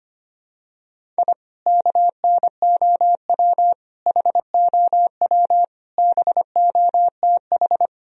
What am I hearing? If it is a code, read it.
Morse code, heard as a keyed tone, about 25 wpm, 700 Hz. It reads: I KNOW HOW BOTH
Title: Sentences from Top 1000 Words - Review Part 1 - 25wpm